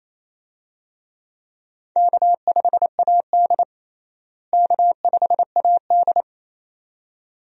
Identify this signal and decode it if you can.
Morse code, heard as a keyed tone, about 28 wpm, 700 Hz. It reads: K5AD K5AD
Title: USA Call Signs - 4 Characters - Repeated 2x - 28wpm